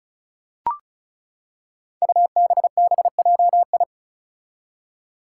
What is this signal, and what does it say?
Morse code, heard as a keyed tone, about 34 wpm, 700 Hz. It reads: UBBJI
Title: Five Challenging Letter-Number Combinations 35wpm